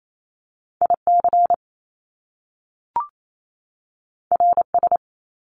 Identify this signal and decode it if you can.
Morse code, heard as a keyed tone, about 28 wpm, 700 Hz. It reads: IC RS
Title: Top Two Letter Combinations in Words 28wpm